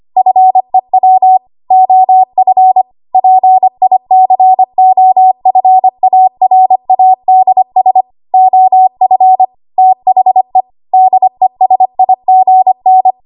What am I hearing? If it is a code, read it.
Morse code, heard as a keyed tone, about 25 wpm, 750 Hz. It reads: FEW OF PICOFARADS OF THE DESIGN
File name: Archive / 25 WPM / 25WPM